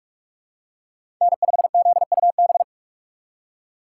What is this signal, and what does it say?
Morse code, heard as a keyed tone, about 44 wpm, 700 Hz. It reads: N5ZUB
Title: USA Call Signs - 5 Characters - 6x Character Spacing - 45wpm